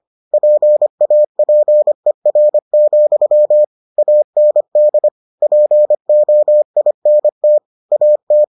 Morse code, heard as a keyed tone, about 25 wpm, 600 Hz. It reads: PAPER, AND POINT AT